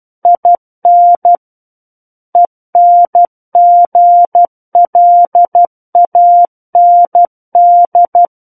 Morse code, heard as a keyed tone, about 12 wpm, 700 Hz. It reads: IN ENGLAND
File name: Book One / 03-Chapter 3 - 12 WPM